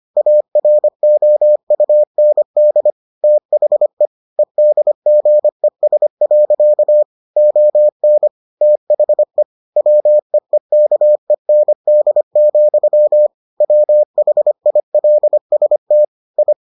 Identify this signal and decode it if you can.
Morse code, heard as a keyed tone, about 25 wpm, 600 Hz. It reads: AROUND THE EDGES. ON THE WEEKEND, WHILST I